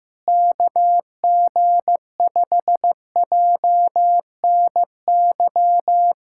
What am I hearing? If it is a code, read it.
Morse code, heard as a keyed tone, about 15 wpm, 700 Hz. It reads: KG5JNY
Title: USA Callsign DE Callsign - 6 Characters - 15wpm